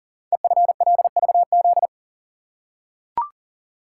Morse code, heard as a keyed tone, about 42 wpm, 700 Hz. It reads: EFLVZ
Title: Code Groups - 5 Letters - 40wpm